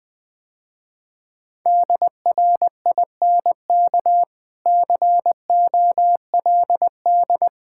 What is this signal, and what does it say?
Morse code, heard as a keyed tone, about 20 wpm, 700 Hz. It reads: DRINK COLD
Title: Easy Sentences with Hints - 20wpm